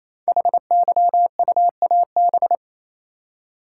Morse code, heard as a keyed tone, about 29 wpm, 700 Hz. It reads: HYUAB
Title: Code Groups - 5 Letter:Numbers - 28wpm